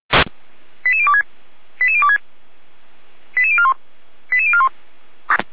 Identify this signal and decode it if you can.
Five-tone selective call (ZVEI), heard as quick runs of five tones, tones of about 70 ms. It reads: 88026 88026 88041 88041